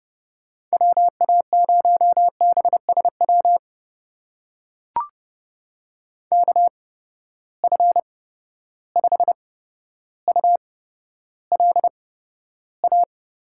Morse code, heard as a keyed tone, about 30 wpm, spaced as for 13 wpm, 700 Hz. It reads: WA0BSW KF5ULA
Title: USA Call Signs - 6 Characters - 8x Character Spacing - 30wpm